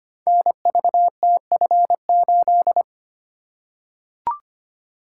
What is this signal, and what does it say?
Morse code, heard as a keyed tone, about 25 wpm, 700 Hz. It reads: NVTF8
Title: Code Groups - 5 Letter:Numbers - 25wpm